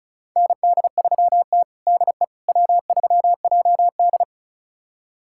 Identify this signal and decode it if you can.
Morse code, heard as a keyed tone, about 36 wpm, 700 Hz. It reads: ND3T DE W3JD
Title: USA Callsign DE Callsign - 4 Characters - 35wpm